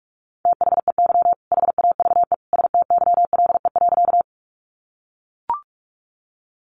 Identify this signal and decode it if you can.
Morse code, heard as a keyed tone, about 44 wpm, 700 Hz. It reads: THEY HAVE STYLE.
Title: Sentences from Top 800 Words 45wpm